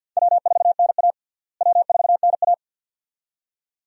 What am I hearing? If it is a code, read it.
Morse code, heard as a keyed tone, about 53 wpm, 700 Hz. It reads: W4NA W4NA
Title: USA Call Signs - 4 Characters - Repeated 2x - 50wpm